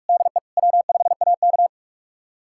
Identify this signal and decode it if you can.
Morse code, heard as a keyed tone, about 46 wpm, 700 Hz. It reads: DE W5AK
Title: USA Callsign DE Callsign 45wpm